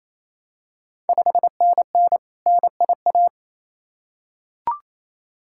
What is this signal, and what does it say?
Morse code, heard as a keyed tone, about 27 wpm, 690 Hz. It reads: 5NN NIA
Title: Ontario QSO Party Exchange 28wpm